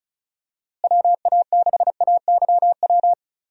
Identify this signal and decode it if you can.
Morse code, heard as a keyed tone, about 34 wpm, 700 Hz. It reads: WA6AYW
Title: USA Call Signs - 6 Characters - Repeated 2x - 35wpm